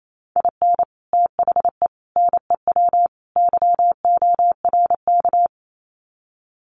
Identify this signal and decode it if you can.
Morse code, heard as a keyed tone, about 28 wpm, 700 Hz. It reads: IN THE NEW YORK
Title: Sets of 4 Words 28wpm